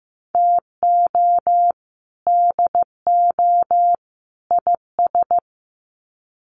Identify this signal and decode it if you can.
Morse code, heard as a keyed tone, about 15 wpm, 700 Hz. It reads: TO DO IS
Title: Sets of 3 Words 15wpm